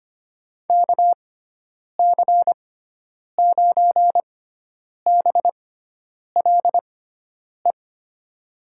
Morse code, heard as a keyed tone, about 25 wpm, spaced as for 9 wpm, 700 Hz. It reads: KC9BLE
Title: USA Call Signs - 6 Characters - 6x Character Spacing - 25wpm